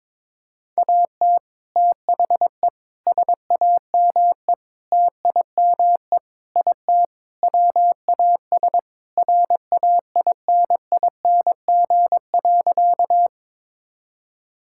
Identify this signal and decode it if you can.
Morse code, heard as a keyed tone, about 21 wpm, 700 Hz. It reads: AT THE SAME TIME IT WAS RAINING.
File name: practice / 22wpm / Sentences with 3-Word Phrases - 22wpm